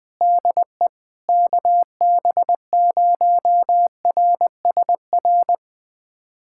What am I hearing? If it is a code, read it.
Morse code, heard as a keyed tone, about 21 wpm, 700 Hz. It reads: DE KB0RSR
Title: USA Callsign DE Callsign - 6 Characters - 20wpm